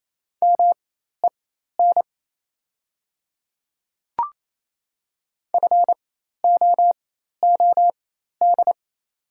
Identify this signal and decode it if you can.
Morse code, heard as a keyed tone, about 27 wpm, spaced as for 13 wpm, 700 Hz. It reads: MEN FOOD
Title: 4x Character Spacing - Top 300 Words - 28wpm